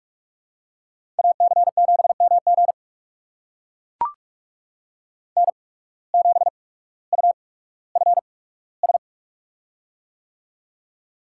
Morse code, heard as a keyed tone, about 44 wpm, spaced as for 19 wpm, 700 Hz. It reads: AC7MG N7UFS
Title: USA Call Signs - 5 Characters - 8x Character Spacing - 45wpm